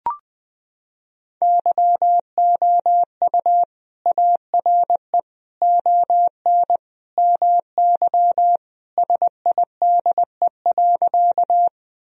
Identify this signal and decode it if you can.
Morse code, heard as a keyed tone, about 20 wpm, 700 Hz. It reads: YOU ARE ON MY SIDE.
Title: Sentences from Top 1000 Words - Review Part 1 - 20wpm